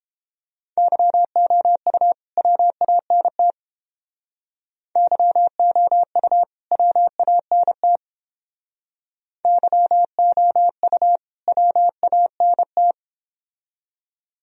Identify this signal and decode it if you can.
Morse code, heard as a keyed tone, about 29 wpm, 700 Hz. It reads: YOU WANT YOU WANT YOU WANT